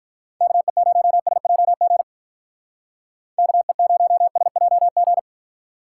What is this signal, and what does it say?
Morse code, heard as a keyed tone, about 45 wpm, 700 Hz. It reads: KE0SJG KE0SJG